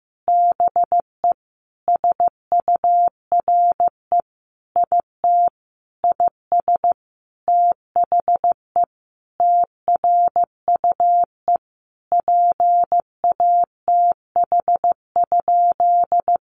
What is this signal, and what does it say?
Morse code, heard as a keyed tone, about 15 wpm, 700 Hz. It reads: BE SURE IT IS THE TRUE PATH?
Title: Sentences from Top 1400 Words 15wpm